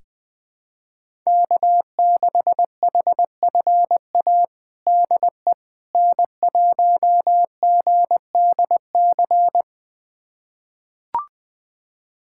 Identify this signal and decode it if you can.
Morse code, heard as a keyed tone, about 21 wpm, 700 Hz. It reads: K6HFA DE N1GDC
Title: USA Callsign DE Callsign - 5 Characters - 20wpm